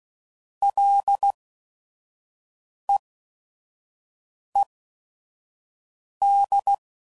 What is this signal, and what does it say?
Morse code, heard as a keyed tone, about 16 wpm, spaced as for 5 wpm, 800 Hz. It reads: LEED
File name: Prefixes / FLs & THs / Single L's